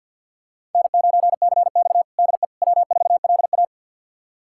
Morse code, heard as a keyed tone, about 49 wpm, 700 Hz. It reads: N9CX DE W4LA
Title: USA Callsign DE Callsign - 4 Characters - 50wpm